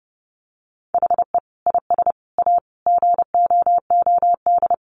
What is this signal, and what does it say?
Morse code, heard as a keyed tone, about 28 wpm, 700 Hz. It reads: HE IS A GOOD
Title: Sentences from Top 1100 Words 30wpm